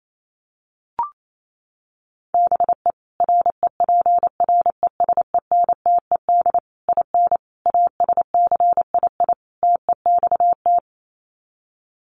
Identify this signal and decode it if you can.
Morse code, heard as a keyed tone, about 28 wpm, 700 Hz. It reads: BE REPRESENTED IN ASCII TEXT
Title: Sets of 5 Words 28wpm